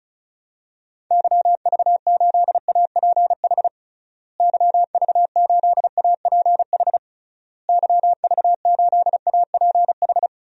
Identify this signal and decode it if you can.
Morse code, heard as a keyed tone, about 34 wpm, 700 Hz. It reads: YV8APH YV8APH YV8APH
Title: DXCC-100 Call Signs - 6 Characters - Repeated 3x - 35wpm